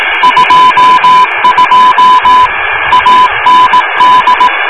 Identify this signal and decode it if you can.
Morse code, heard as a keyed tone, about 18 wpm, 1000 Hz, with radio noise. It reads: 22 AND